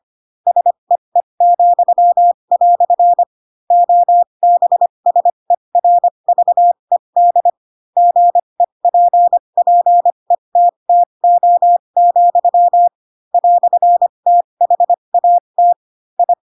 Morse code, heard as a keyed tone, about 25 wpm, spaced as for 22 wpm, 700 Hz. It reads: SEE," OBSERVED GEPPETTO, "THAT I
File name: Pinocchio / Pinocchio ch7 18 25 ik1hge